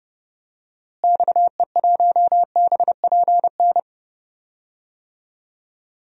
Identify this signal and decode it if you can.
Morse code, heard as a keyed tone, about 31 wpm, 700 Hz. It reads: XE1BPN
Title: DXCC-100 Call Signs - 6 Characters - Repeated 2x - 30wpm